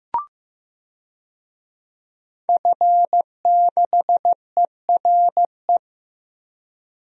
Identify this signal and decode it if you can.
Morse code, heard as a keyed tone, about 15 wpm, 700 Hz. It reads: F6ERE
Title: DXCC-100 Call Signs - 5 Characters - Repeated 3x - 15wpm